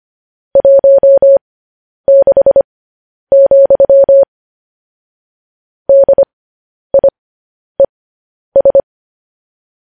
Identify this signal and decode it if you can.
Morse code, heard as a keyed tone, about 25 wpm, spaced as for 10 wpm, 550 Hz. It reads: 16, DIES
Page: news-headlines-in-morse-code-at-10-wpm